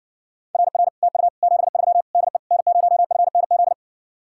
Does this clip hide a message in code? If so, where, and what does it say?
Yes: Morse code from 0.5 to 3.7 s, heard as a keyed tone, about 60 wpm, 700 Hz. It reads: RR TU 73 DE N0FNZ